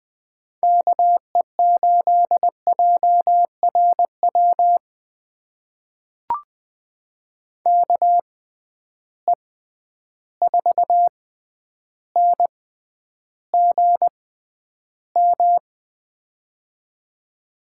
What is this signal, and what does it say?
Morse code, heard as a keyed tone, about 20 wpm, spaced as for 10 wpm, 700 Hz. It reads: KE8JRW KE4NGM
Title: USA Call Signs - 6 Characters - 6x Character Spacing - 20wpm